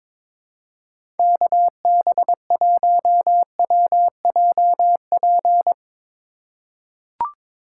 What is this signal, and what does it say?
Morse code, heard as a keyed tone, about 22 wpm, 700 Hz. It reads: KB1WJP
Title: USA Call Signs - 6 Characters - 2x Character Spacing - 22wpm